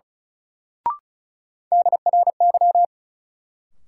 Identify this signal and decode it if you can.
Morse code, heard as a keyed tone, about 34 wpm, 700 Hz. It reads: DRY
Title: Sentences from Top 500 Words - New Word Drill - 35wpm